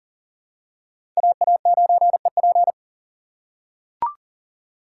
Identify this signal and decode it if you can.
Morse code, heard as a keyed tone, about 40 wpm, 700 Hz. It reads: AA9EP
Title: USA Call Signs - 5 Characters - 3x Character Spacing - 40wpm